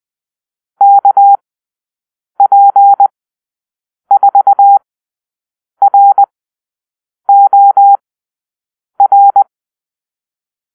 Morse code, heard as a keyed tone, about 20 wpm, spaced as for 7 wpm, 800 Hz. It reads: KP4ROR
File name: arrl / Call-Sign Drill-9